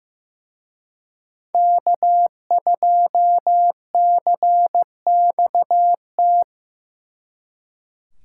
Morse code, heard as a keyed tone, about 18 wpm, 700 Hz. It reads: K2CXT
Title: Speed Racing - USA 1x3 Call Signs 15wpm